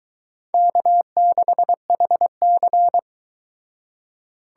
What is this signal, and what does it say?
Morse code, heard as a keyed tone, about 24 wpm, 700 Hz. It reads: K6HC